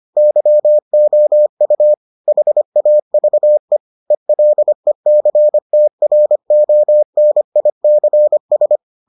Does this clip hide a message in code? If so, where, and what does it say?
Yes: Morse code from 0.2 to 8.8 s, heard as a keyed tone, about 25 wpm, 600 Hz. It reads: YOU HAVE ELECTRONICS